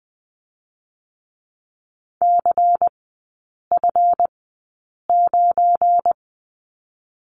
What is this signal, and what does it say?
Morse code, heard as a keyed tone, about 20 wpm, 700 Hz. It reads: C F 9
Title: 205 Lesson 33e - Letter j - Three Characters 20wpm